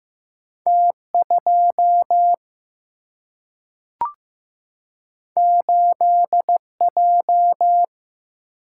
Morse code, heard as a keyed tone, about 15 wpm, 700 Hz. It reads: T 2 8 J
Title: Single-Letter Single-Number 15wpm